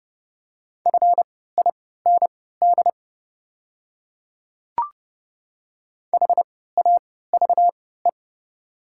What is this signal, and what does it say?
Morse code, heard as a keyed tone, about 32 wpm, spaced as for 17 wpm, 700 Hz. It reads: FIND HAVE
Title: 3x Character Spacing - Top 100 Words - 30wpm